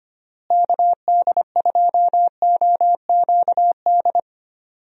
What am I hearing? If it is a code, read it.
Morse code, heard as a keyed tone, about 25 wpm, 700 Hz. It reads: KD2OQD
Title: USA Call Signs - 6 Characters - Repeated 3x - 25wpm